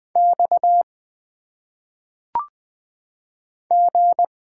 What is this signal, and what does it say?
Morse code, heard as a keyed tone, about 20 wpm, 700 Hz. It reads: X G
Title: Single Letters - Phonetic Pronunciation 20wpm